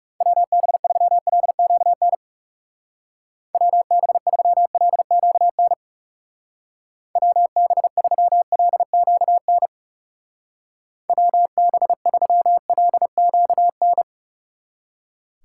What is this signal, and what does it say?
Morse code, heard as a keyed tone, about 36 wpm, 700 Hz. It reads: WB3LQN WB3LQN WB3LQN WB3LQN
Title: Speed Racing - USA 2x3 Call Signs 30wpm